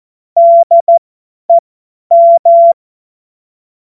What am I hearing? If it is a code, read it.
Morse code, heard as a keyed tone, about 14 wpm, spaced as for 10 wpm, 680 Hz. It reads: DEM
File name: Morsekurs / Witze / Witz 17. Mai 2022 50 F70